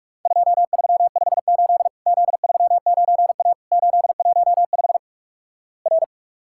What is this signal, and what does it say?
Morse code, heard as a keyed tone, about 44 wpm, 700 Hz. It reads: 2358 Z39A 815 R
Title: Worked All Europe DX Contest - QTC Exchange - 45wpm